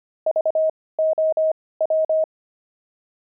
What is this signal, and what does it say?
Morse code, heard as a keyed tone, about 25 wpm, spaced as for 18 wpm, 630 Hz. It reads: VOW
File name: pangram-fs-18-cs-25-sphinx